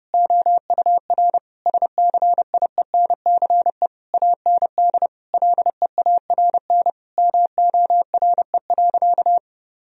Morse code, heard as a keyed tone, about 30 wpm, 700 Hz. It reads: OUR SCIENCE AND LEARN MORE.
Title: Sentences from Top 1000 Words - Review Part 4 - 30wpm